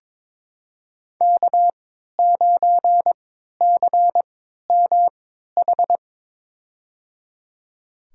Morse code, heard as a keyed tone, about 22 wpm, spaced as for 12 wpm, 700 Hz. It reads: K9CMH
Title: USA Call Signs - 5 Characters - 3x Character Spacing - 22wpm